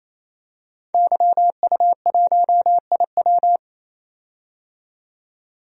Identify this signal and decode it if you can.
Morse code, heard as a keyed tone, about 28 wpm, 700 Hz. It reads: YU1IW
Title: DXCC-100 Call Signs - 5 Characters - 28wpm